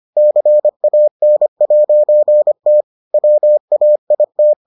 Morse code, heard as a keyed tone, about 25 wpm, 600 Hz. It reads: CAN'T WAIT